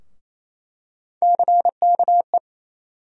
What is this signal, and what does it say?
Morse code, heard as a keyed tone, about 27 wpm, 700 Hz. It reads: CKE
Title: USA Call Sign Postfixes - 3 Letters 28wpm